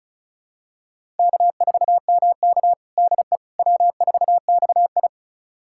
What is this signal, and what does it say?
Morse code, heard as a keyed tone, about 36 wpm, 700 Hz. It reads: K4MK DE W4XI
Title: USA Callsign DE Callsign - 4 Characters - 35wpm